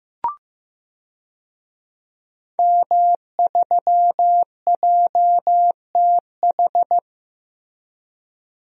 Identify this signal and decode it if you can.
Morse code, heard as a keyed tone, about 15 wpm, 700 Hz. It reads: M3JTH
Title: DXCC-100 Call Signs - 5 Characters - 15wpm